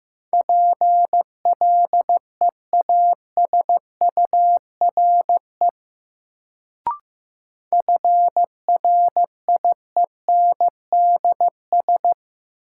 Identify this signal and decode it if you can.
Morse code, heard as a keyed tone, about 15 wpm, 700 Hz. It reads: PLEASURE FRIENDS
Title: Sentences from Top 1600 Words - New Word Drill Alt - 15wpm